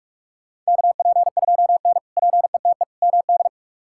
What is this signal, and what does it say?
Morse code, heard as a keyed tone, about 45 wpm, 700 Hz. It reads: KP2N PETE MD
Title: K1USN SST QSO Practice - 45wpm